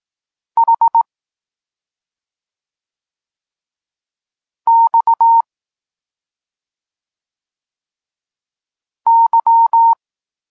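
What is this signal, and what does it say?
Morse code, heard as a keyed tone, about 18 wpm, 920 Hz. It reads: H X Y
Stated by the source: Polish Morse lesson teaching the letter C, from SP9MOA Niepołomicki Klub Krótkofalowców.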